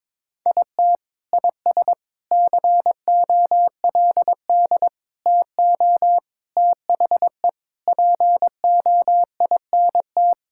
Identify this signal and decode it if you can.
Morse code, heard as a keyed tone, about 22 wpm, 700 Hz. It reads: IT IS COLD TO THE POINT
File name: practice / 22wpm / Sentences with 4-Word Phrases - 22wpm